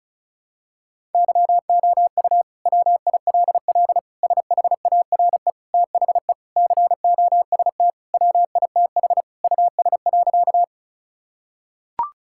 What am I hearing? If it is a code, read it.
Morse code, heard as a keyed tone, about 34 wpm, 700 Hz. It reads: YOU WILL SHARE THE COST WITH US.
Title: Sentences from Top 1000 Words - Review Part 6 - 35wpm